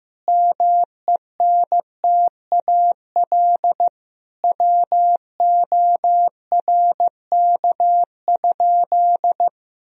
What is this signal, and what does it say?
Morse code, heard as a keyed tone, about 15 wpm, 700 Hz. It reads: MENTAL WORK?
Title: Sentences from Top 1000 Words - Review Part 4 - 15wpm